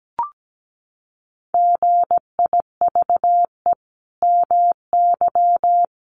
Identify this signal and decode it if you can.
Morse code, heard as a keyed tone, about 17 wpm, 700 Hz. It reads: GIVE MY
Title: Sentences from Top 1400 Words 17wpm